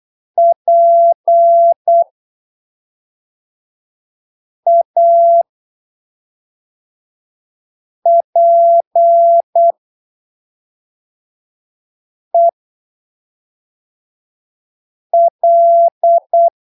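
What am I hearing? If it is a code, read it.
Morse code, heard as a keyed tone, about 8 wpm, spaced as for 3 wpm, 680 Hz. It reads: PAPEL
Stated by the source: Morse training text at 8 ppm for Brazilian class B amateur radio telegraphy exams, from Portal do CW, PY2BRL.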